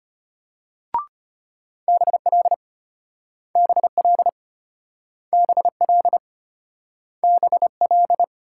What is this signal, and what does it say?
Morse code, heard as a keyed tone, about 33 wpm, 700 Hz. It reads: BL BL BL BL